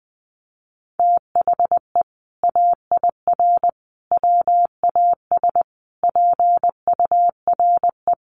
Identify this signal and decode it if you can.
Morse code, heard as a keyed tone, about 20 wpm, 700 Hz. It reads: THE AIR WAS PURE
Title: Sentences from Top 1500 Words 20wpm